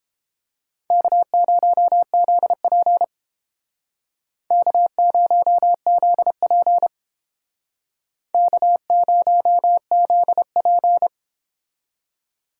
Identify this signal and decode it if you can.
Morse code, heard as a keyed tone, about 31 wpm, 700 Hz. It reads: K0ZP K0ZP K0ZP